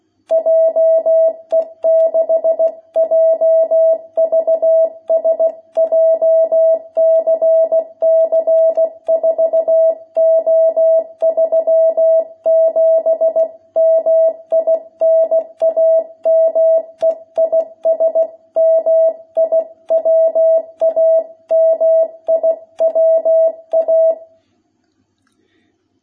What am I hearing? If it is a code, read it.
Morse code, heard as a keyed tone, about 16 wpm, 660 Hz. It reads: JE6JVS JCC 4O37 MI NAME IS MIWA MIWA
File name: jh8cbh / 2021cw